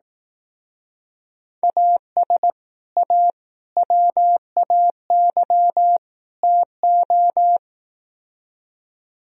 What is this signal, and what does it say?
Morse code, heard as a keyed tone, about 18 wpm, 700 Hz. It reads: AS A WAY TO
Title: Speed-Racing - Sets of 4 Words 15wpm